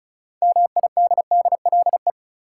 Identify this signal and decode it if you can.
Morse code, heard as a keyed tone, about 34 wpm, 700 Hz. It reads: MIDDLE
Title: Six Letter Words - 35wpm